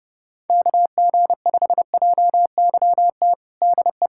Morse code, heard as a keyed tone, about 30 wpm, 700 Hz. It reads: KG5JYT DE
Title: USA Callsign DE Callsign 30wpm